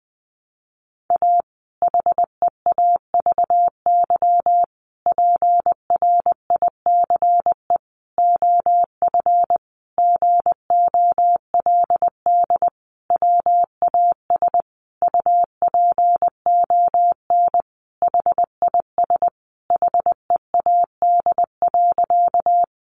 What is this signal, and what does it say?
Morse code, heard as a keyed tone, about 20 wpm, 700 Hz. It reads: A HEAVY PRICE OF GOLD WAS UPON HIS HEAD.
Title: Sentences from Top 1100 Words 20wpm